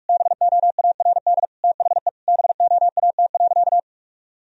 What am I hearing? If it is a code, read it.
Morse code, heard as a keyed tone, about 46 wpm, 700 Hz. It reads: BOARD THE BOAT.
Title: Sentences from Top 1000 Words - Review Part 4 - 45wpm